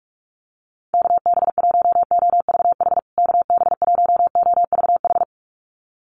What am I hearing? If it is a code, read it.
Morse code, heard as a keyed tone, about 44 wpm, 700 Hz. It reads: KB1OVH KB1OVH
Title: USA Call Signs Repeated 2x 45wpm